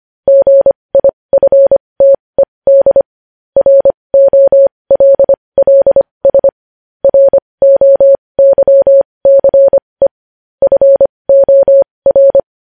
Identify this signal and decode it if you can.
Morse code, heard as a keyed tone, about 25 wpm, spaced as for 20 wpm, 550 Hz. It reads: GIFTED ROLLS ROYCE FOR
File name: podcast / News headlines-1x-25x20